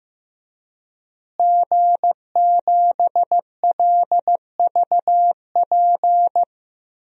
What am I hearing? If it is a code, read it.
Morse code, heard as a keyed tone, about 15 wpm, 700 Hz. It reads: G7LVP
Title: DXCC-100 Call Signs - 5 Characters - Repeated 2x - 15wpm